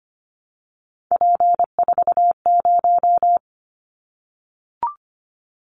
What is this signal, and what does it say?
Morse code, heard as a keyed tone, about 25 wpm, 700 Hz. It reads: P40
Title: Callsign Prefixes for Top 100 DXCC - 25wpm